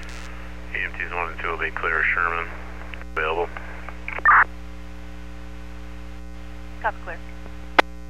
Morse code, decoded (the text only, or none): none